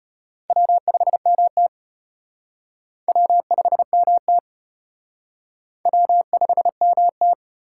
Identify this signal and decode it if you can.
Morse code, heard as a keyed tone, about 33 wpm, 700 Hz. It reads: W5MT W5MT W5MT